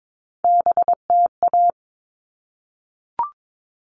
Morse code, heard as a keyed tone, about 22 wpm, 700 Hz. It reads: BTA
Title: All Cut Numbers - 3-Digit 22wpm